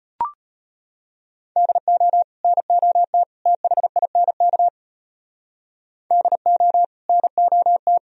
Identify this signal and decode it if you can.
Morse code, heard as a keyed tone, about 33 wpm, 700 Hz. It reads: DO NOT THINK DO NOT